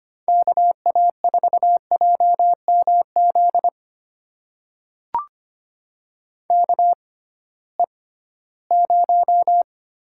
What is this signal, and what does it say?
Morse code, heard as a keyed tone, about 25 wpm, spaced as for 16 wpm, 700 Hz. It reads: KA4JMZ KE0
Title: USA Call Signs - 6 Characters - 6x Character Spacing - 25wpm